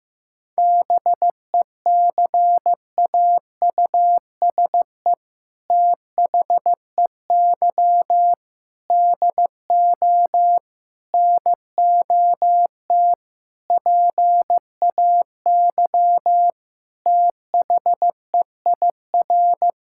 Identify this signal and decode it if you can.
Morse code, heard as a keyed tone, about 15 wpm, 700 Hz. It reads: BECAUSE THEY DO NOT PAY THEIR
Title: Sentences from Top 1000 Words - Review Part 5 - 15wpm